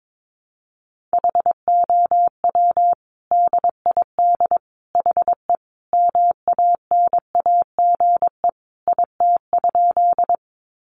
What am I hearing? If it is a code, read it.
Morse code, heard as a keyed tone, about 22 wpm, 700 Hz. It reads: HOW DID HE MANAGE IT?
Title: Sentences from Top 1400 Words 22wpm